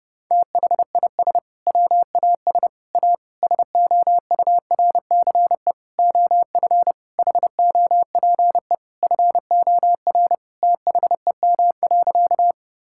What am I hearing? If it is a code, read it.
Morse code, heard as a keyed tone, about 30 wpm, 700 Hz. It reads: THIS WAS A SOURCE OF HOPE FOR THEM.
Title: Sentences from Top 1000 Words - Review Part 5 - 30wpm